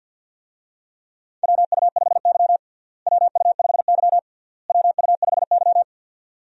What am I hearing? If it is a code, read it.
Morse code, heard as a keyed tone, about 51 wpm, 700 Hz. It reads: WU5Y WU5Y WU5Y
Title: USA Call Signs - 4 Characters - Repeated 3x - 50wpm